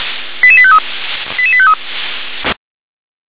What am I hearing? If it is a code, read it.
Five-tone selective call (ZVEI), heard as quick runs of five tones, tones of about 70 ms. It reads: 88853 88853